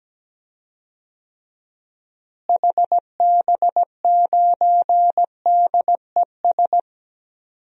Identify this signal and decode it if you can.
Morse code, heard as a keyed tone, about 17 wpm, 700 Hz. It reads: HB9DES
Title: DXCC-100 Call Signs - 6 Characters - Repeated 2x - 17wpm